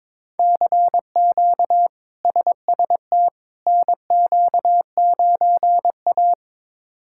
Morse code, heard as a keyed tone, about 21 wpm, 700 Hz. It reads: CQ SST NQ9A
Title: K1USN SST QSO Practice - 22wpm